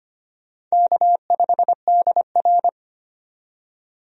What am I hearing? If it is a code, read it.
Morse code, heard as a keyed tone, about 25 wpm, 700 Hz. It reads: K5DR
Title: USA Call Signs - 4 Characters - 25wpm